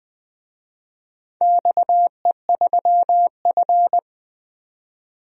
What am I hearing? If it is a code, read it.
Morse code, heard as a keyed tone, about 20 wpm, 700 Hz. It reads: XE3F
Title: DXCC-100 Call Signs - 4 Characters - 20wpm